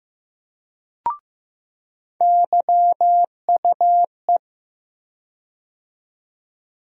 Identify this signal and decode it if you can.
Morse code, heard as a keyed tone, about 15 wpm, 700 Hz. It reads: YUE
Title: USA Call Sign Postfixes - 3 Letters 15wpm